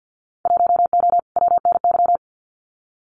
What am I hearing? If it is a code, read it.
Morse code, heard as a keyed tone, about 50 wpm, 700 Hz. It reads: 1O WNY
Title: Winter Field Day Exchanges Encore 50wpm